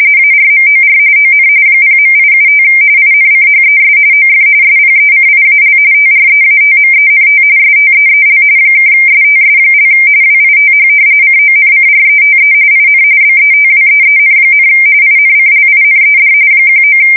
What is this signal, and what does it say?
RTTY, about 45.45 baud. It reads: ARMOR ON 0.875 INCHES OF SPECIAL TREATMENT STEEL (STS) WITH CONCRETE IN BETWEEN, INCLINED 19
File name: fsk170 / battleshipnj